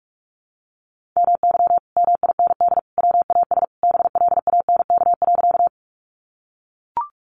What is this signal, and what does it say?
Morse code, heard as a keyed tone, about 44 wpm, 700 Hz. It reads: MY MIND WAS BLANK.
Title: Sentences from Top 1500 Words 45wpm